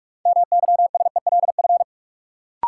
Morse code, heard as a keyed tone, about 42 wpm, 700 Hz. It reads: MYSELF
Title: Sentences from Top 1300 Words - New Word Drill Alt - 45wpm